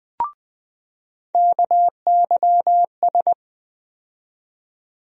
Morse code, heard as a keyed tone, about 20 wpm, 700 Hz. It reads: KYS
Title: USA Call Sign Postfixes - 3 Letters 20wpm